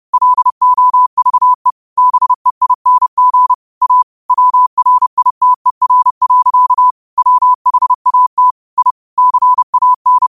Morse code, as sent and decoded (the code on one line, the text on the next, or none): .-.. --- ...- . / -... .. -. --. / .- / .-- .-. .. - . .-. .-.-.- / .-- .... .- - / .. / -.-. .- -.
LOVE BEING A WRITER. WHAT I CAN